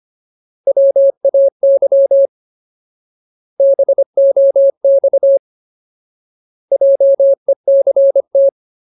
Morse code, heard as a keyed tone, about 25 wpm, 550 Hz. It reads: WAY BOX JECT
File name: Ultimate Word List / 25WPM Ultimate